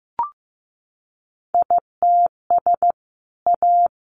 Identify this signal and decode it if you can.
Morse code, heard as a keyed tone, about 15 wpm, 700 Hz. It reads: ITS A
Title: Sentences from Top 600 Words 15wpm